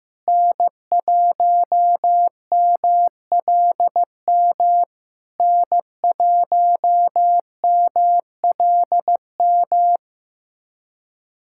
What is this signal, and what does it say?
Morse code, heard as a keyed tone, about 14 wpm, 700 Hz. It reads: N1MLM N1MLM
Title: SOTA QSO Practice 15wpm